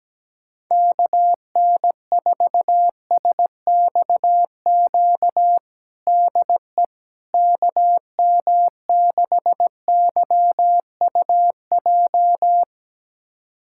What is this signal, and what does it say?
Morse code, heard as a keyed tone, about 17 wpm, 700 Hz. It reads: KN4SXQ DE KM6YUJ
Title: USA Callsign DE Callsign 17wpm